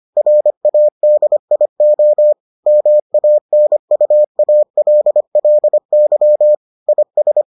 Morse code, heard as a keyed tone, about 25 wpm, 600 Hz. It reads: RADIO MANUALLY IS